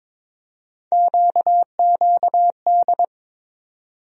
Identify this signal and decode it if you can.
Morse code, heard as a keyed tone, about 22 wpm, 700 Hz. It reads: QQD
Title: USA Call Sign Postfixes - 3 Letters 22wpm